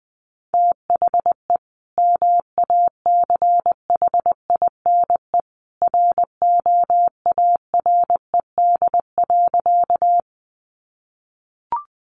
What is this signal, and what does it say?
Morse code, heard as a keyed tone, about 20 wpm, 700 Hz. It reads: THE MACHINE ROARED.
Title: A Sound Of Thunder 20wpm